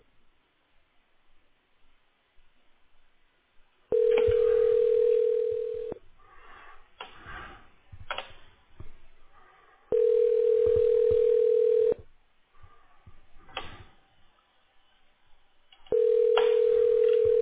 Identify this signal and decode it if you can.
Ringback tone (North America), 440 + 480 Hz.